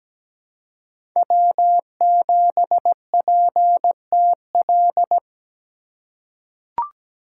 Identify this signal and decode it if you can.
Morse code, heard as a keyed tone, about 17 wpm, 700 Hz. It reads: W7PTL
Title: USA Call Signs - 5 Characters - 4x Character Spacing - 17wpm